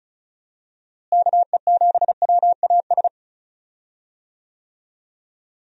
Morse code, heard as a keyed tone, about 36 wpm, 700 Hz. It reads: KE7WAS
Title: USA Call Signs - 6 Characters - Repeated 3x - 35wpm